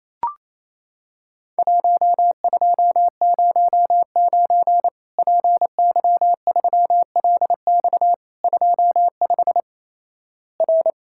Morse code, heard as a keyed tone, about 28 wpm, 700 Hz. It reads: 1209 PY3LX 25 R
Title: Worked All Europe DX Contest - QTC Exchange - 28wpm